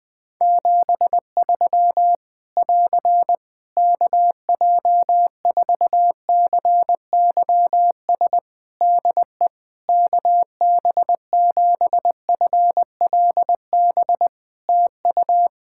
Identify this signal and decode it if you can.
Morse code, heard as a keyed tone, about 20 wpm, 700 Hz. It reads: 73 <AR> KJ4CYS DE KB7FLB TU
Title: Practice QSO - Exchange 4 - Ending - 20wpm